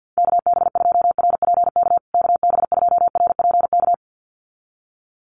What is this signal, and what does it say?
Morse code, heard as a keyed tone, about 50 wpm, 700 Hz. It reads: KB2RPK KB2RPK
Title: USA Call Signs Repeated 2x 50wpm